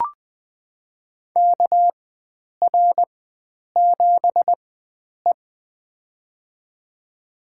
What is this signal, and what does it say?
Morse code, heard as a keyed tone, about 21 wpm, spaced as for 9 wpm, 700 Hz. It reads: KR7E
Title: USA Call Signs - 4 Characters - 4x Character Spacing - 20wpm